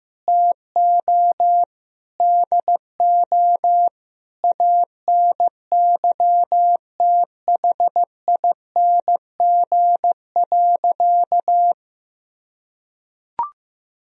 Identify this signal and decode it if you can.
Morse code, heard as a keyed tone, about 15 wpm, 700 Hz. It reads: TO DO ANYTHING.
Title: Sentences from Top 1000 Words - Review Part 2 - 15wpm